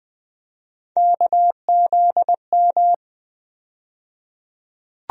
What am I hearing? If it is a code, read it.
Morse code, heard as a keyed tone, about 20 wpm, 700 Hz. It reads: KZM
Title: USA Call Sign Postfixes - 3 Letters 20wpm